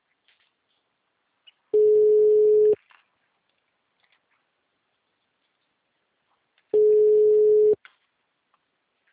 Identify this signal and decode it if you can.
Ringback tone (Europe, 425 Hz).
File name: out- -2034- - - .4578